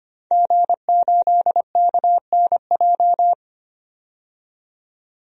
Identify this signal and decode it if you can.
Morse code, heard as a keyed tone, about 25 wpm, 700 Hz. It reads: G8KNJ
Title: DXCC-100 Call Signs - 5 Characters - 25wpm